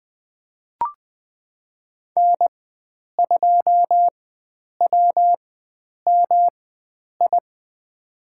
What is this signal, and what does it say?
Morse code, heard as a keyed tone, about 20 wpm, spaced as for 9 wpm, 700 Hz. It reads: N2WMI
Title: USA Call Signs - 4x Character Spacing - 20wpm